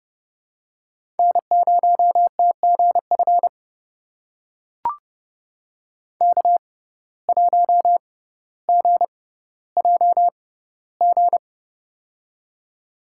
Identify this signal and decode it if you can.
Morse code, heard as a keyed tone, about 32 wpm, spaced as for 16 wpm, 700 Hz. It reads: N0TGF K1GJG
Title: USA Call Signs - 5 Characters - 6x Character Spacing - 30wpm